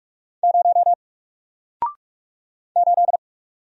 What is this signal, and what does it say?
Morse code, heard as a keyed tone, about 44 wpm, 700 Hz. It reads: 0 8
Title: Single Character (Letter:Number:Punctuation:Prosign) Alt 45wpm